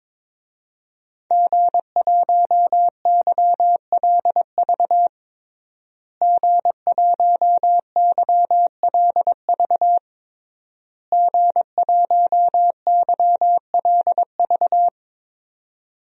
Morse code, heard as a keyed tone, about 22 wpm, 700 Hz. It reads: G1YLV G1YLV G1YLV
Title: DXCC-100 Call Signs - 5 Characters - Repeated 3x - 22wpm